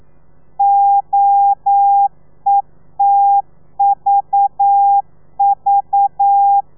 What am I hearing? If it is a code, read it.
Morse code, heard as a keyed tone, about 9 wpm, 800 Hz. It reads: OETVV
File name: telegrafia / i2tfj / Lez4 40